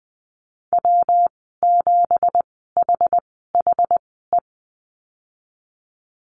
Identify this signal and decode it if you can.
Morse code, heard as a keyed tone, about 20 wpm, spaced as for 15 wpm, 700 Hz. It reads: W7HHE
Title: 219 Lesson 35g - Number 0 - Call Signs 20wpm